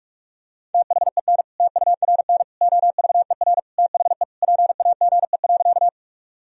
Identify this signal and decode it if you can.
Morse code, heard as a keyed tone, about 46 wpm, 700 Hz. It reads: THEN TURN OVER THE PAGE.
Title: Sentences from Top 1000 Words - Review Part 1 - 45wpm